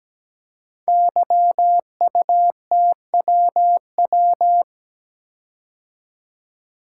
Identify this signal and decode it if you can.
Morse code, heard as a keyed tone, about 17 wpm, 700 Hz. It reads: YUTWW
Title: Code Groups - 5 Letters - 17wpm